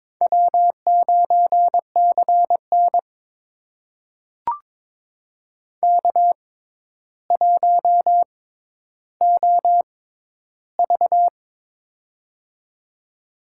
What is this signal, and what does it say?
Morse code, heard as a keyed tone, about 22 wpm, spaced as for 11 wpm, 700 Hz. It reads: W9CN K1OV
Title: USA Call Signs - 6x Character Spacing - 22wpm